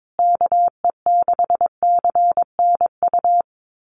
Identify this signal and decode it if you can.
Morse code, heard as a keyed tone, about 22 wpm, 700 Hz. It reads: KE6CNU
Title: USA Call Signs Repeated 2x 22wpm